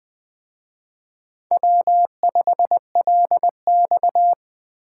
Morse code, heard as a keyed tone, about 20 wpm, 700 Hz. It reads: W5LX
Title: Speed Racing - USA 1x2 Call Signs 17wpm